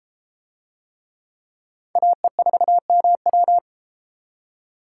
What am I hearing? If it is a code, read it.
Morse code, heard as a keyed tone, about 30 wpm, 700 Hz. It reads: AE4MW